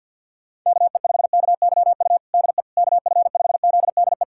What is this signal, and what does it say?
Morse code, heard as a keyed tone, about 51 wpm, 690 Hz. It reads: KE5KYU DE KF5ZDE
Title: USA Callsign DE Callsign 50wpm